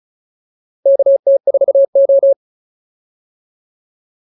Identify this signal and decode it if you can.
Morse code, heard as a keyed tone, about 35 wpm, 550 Hz. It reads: KT4O